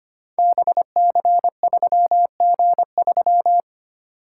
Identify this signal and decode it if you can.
Morse code, heard as a keyed tone, about 25 wpm, 700 Hz. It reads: BC3G3
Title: Code Groups - 5 Letter:Numbers - 25wpm